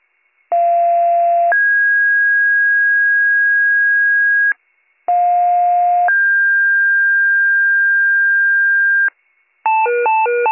Two-tone page, heard as a one-second tone, then a three-second tone, about 690 then 1650 Hz, about 710 then 1600 Hz. It is heard from 0.5 to 9.1 s.